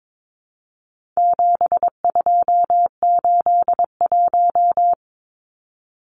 Morse code, heard as a keyed tone, about 22 wpm, 700 Hz. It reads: 7281
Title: Numbers - 4-Digit 22wpm